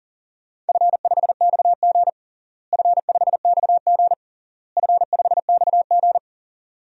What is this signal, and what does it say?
Morse code, heard as a keyed tone, about 37 wpm, 700 Hz. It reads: F5XG F5XG F5XG
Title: DXCC-100 Call Signs - 4 Characters - Repeated 3x - 40wpm